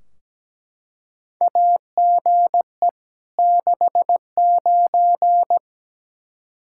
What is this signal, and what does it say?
Morse code, heard as a keyed tone, about 17 wpm, 700 Hz. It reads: AGE 69
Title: Practice QSO - Exchange 3 - More Info - 17wpm